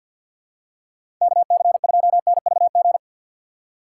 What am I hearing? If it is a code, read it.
Morse code, heard as a keyed tone, about 52 wpm, 700 Hz. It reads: KC2NVG
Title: USA Call Signs - 6 Characters - 3x Character Spacing - 50wpm